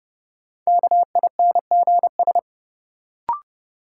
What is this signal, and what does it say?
Morse code, heard as a keyed tone, about 30 wpm, 700 Hz. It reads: KINGS
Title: Sentences from Top 1000 Words - Review Part 6 - New Word Drill - 30wpm